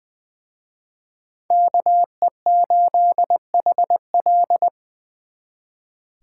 Morse code, heard as a keyed tone, about 24 wpm, 700 Hz. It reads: KE8HL
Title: Speed Racing - USA Call Signs 20wpm